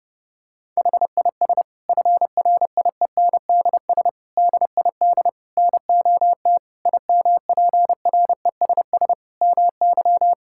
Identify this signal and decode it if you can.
Morse code, heard as a keyed tone, about 30 wpm, 700 Hz. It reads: HIS FRIENDS DID NOT IMPRESS MY
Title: Sentences from Top 1600 Words 30wpm